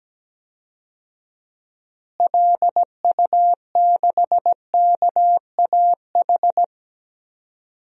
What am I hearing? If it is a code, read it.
Morse code, heard as a keyed tone, about 17 wpm, 700 Hz. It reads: LU6KAH
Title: DXCC-100 Call Signs - 6 Characters - Repeated 2x - 17wpm